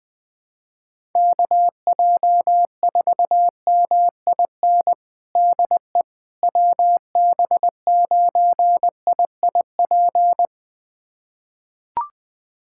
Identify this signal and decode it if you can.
Morse code, heard as a keyed tone, about 20 wpm, 700 Hz. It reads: KJ4MIN DE WB9IIP
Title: USA Callsign DE Callsign - 6 Characters - 20wpm